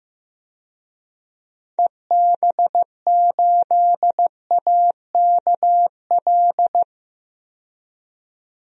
Morse code, heard as a keyed tone, about 15 wpm, 700 Hz. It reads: EB8AKL
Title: DXCC-100 Call Signs - 6 Characters - 15wpm